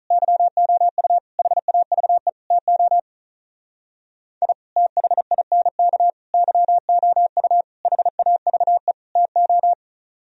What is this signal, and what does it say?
Morse code, heard as a keyed tone, about 43 wpm, 700 Hz. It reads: YOU HAVE TO I THINK YOU HAVE TO
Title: Speed-Racing - Sets of 5 Words 35wpm